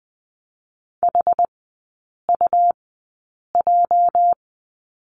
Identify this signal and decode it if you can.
Morse code, heard as a keyed tone, about 20 wpm, spaced as for 8 wpm, 700 Hz. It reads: HUJ
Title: 205 Lesson 33e - Letter j - Three Characters 20wpm